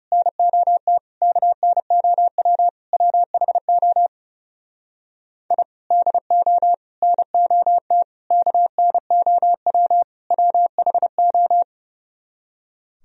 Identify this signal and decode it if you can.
Morse code, heard as a keyed tone, about 40 wpm, 700 Hz. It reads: NOT KNOW WHO I DO NOT KNOW WHO